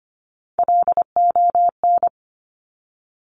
Morse code, heard as a keyed tone, about 25 wpm, 700 Hz. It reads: LON
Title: Top Three Letter Combinations in Words 25wpm